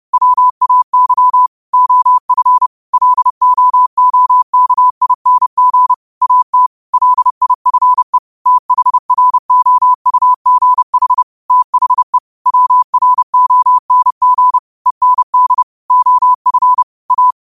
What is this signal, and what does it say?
Morse code, heard as a keyed tone, about 30 wpm, 1000 Hz. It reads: WAY OF LOOKING AT LIFE THROUGH THE WRONG END OF A
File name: qotdfiles / 30wpm